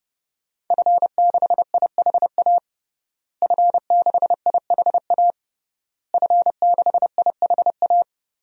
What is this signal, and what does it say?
Morse code, heard as a keyed tone, about 30 wpm, 700 Hz. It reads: F6IHA F6IHA F6IHA
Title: DXCC-100 Call Signs - 5 Characters - Repeated 3x - 30wpm